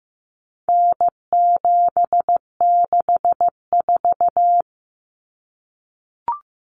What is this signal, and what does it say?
Morse code, heard as a keyed tone, about 15 wpm, 700 Hz. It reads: N764
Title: Common Cut Numbers - 4-Digit 15wpm